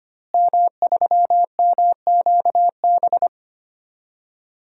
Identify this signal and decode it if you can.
Morse code, heard as a keyed tone, about 25 wpm, 700 Hz. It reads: M3MQB
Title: DXCC-100 Call Signs - 5 Characters - Repeated 3x - 25wpm